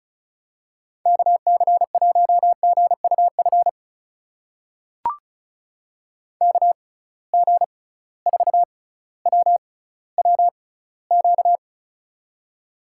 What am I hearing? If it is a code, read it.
Morse code, heard as a keyed tone, about 36 wpm, spaced as for 18 wpm, 700 Hz. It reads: KC1GUF KG4WWQ
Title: USA Call Signs - 6 Characters - 6x Character Spacing - 35wpm